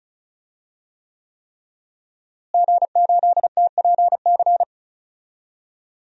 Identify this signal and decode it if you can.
Morse code, heard as a keyed tone, about 36 wpm, 700 Hz. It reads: G8TPC
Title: DXCC-100 Call Signs - 5 Characters - Repeated 3x - 35wpm